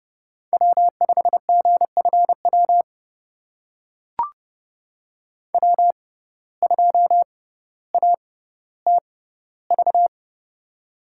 Morse code, heard as a keyed tone, about 29 wpm, spaced as for 15 wpm, 700 Hz. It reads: W5GFW W2ATV
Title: USA Call Signs - 5 Characters - 6x Character Spacing - 30wpm